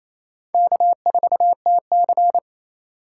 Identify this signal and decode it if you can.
Morse code, heard as a keyed tone, about 28 wpm, 700 Hz. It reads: K4TC
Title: USA Call Signs - 4 Characters - 2x Character Spacing - 28wpm